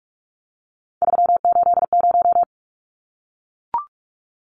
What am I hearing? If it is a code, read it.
Morse code, heard as a keyed tone, about 44 wpm, 700 Hz. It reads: 380
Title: Numbers - 3-Digit 45wpm